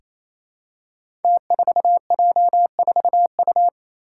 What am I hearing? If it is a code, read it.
Morse code, heard as a keyed tone, about 28 wpm, 700 Hz. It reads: T4J4U
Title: Five Challenging Letter-Number Combinations 28wpm